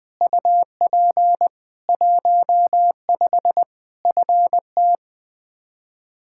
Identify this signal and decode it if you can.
Morse code, heard as a keyed tone, about 20 wpm, 700 Hz. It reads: UP 15 FT
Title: Practice QSO - Exchange 2 - Info - 20wpm